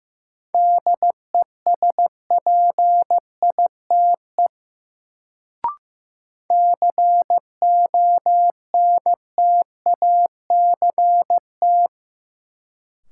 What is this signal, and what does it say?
Morse code, heard as a keyed tone, about 15 wpm, 700 Hz. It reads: DESPITE CONTACT
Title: Sentences from Top 1000 Words - Review Part 7 - New Word Drill - 15wpm